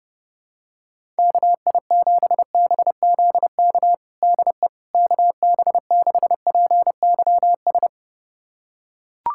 Morse code, heard as a keyed tone, about 30 wpm, 700 Hz. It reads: KI7BZK DE KB6PYS
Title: USA Callsign DE Callsign - 6 Characters - 30wpm